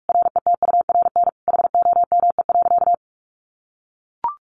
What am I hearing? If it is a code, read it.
Morse code, heard as a keyed tone, about 44 wpm, 700 Hz. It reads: RETURN HOME.
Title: Sentences from Top 800 Words 45wpm